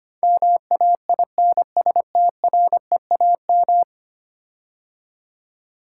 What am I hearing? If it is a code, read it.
Morse code, heard as a keyed tone, about 25 wpm, 700 Hz. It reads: MAINSTREAM
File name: 10 Letter Words - First 3 Letters Given - 25wpm